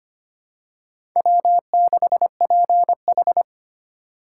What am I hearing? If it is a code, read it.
Morse code, heard as a keyed tone, about 25 wpm, 700 Hz. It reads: W6PH
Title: USA Call Signs - 4 Characters - 4x Character Spacing - 25wpm